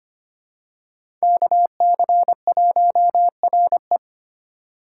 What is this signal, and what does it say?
Morse code, heard as a keyed tone, about 25 wpm, 700 Hz. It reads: KC1RE
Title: USA Call Signs - 5 Characters - 2x Character Spacing - 25wpm